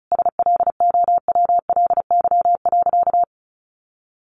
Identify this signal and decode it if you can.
Morse code, heard as a keyed tone, about 34 wpm, 700 Hz. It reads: SLOWLY.
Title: Sentences from Top 1100 Words 35wpm